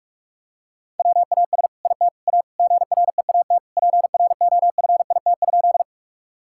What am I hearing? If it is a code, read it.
Morse code, heard as a keyed tone, about 46 wpm, 700 Hz. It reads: WAS IT A GREAT PROFIT?
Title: Sentences from Top 1000 Words - Review Part 5 - 45wpm